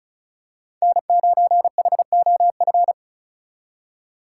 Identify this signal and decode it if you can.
Morse code, heard as a keyed tone, about 36 wpm, 700 Hz. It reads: N9HOF
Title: USA Call Signs - 2x Character Spacing - 35wpm